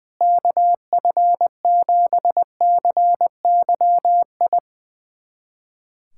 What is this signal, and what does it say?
Morse code, heard as a keyed tone, about 20 wpm, 700 Hz. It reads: KF7CYI
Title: USA Callsign DE Callsign - 6 Characters - 20wpm